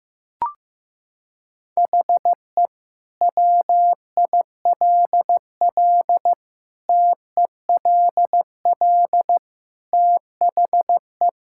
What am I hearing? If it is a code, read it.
Morse code, heard as a keyed tone, about 16 wpm, 700 Hz. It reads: HE WILL TELL THE
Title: Sentences from Top 1000 Words - Review Part 5 - 15wpm